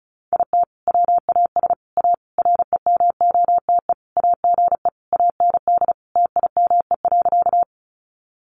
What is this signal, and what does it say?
Morse code, heard as a keyed tone, about 34 wpm, 700 Hz. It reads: IT WAS A REMOTE AGE AND TIME.
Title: Sentences from Top 1300 Words 35wpm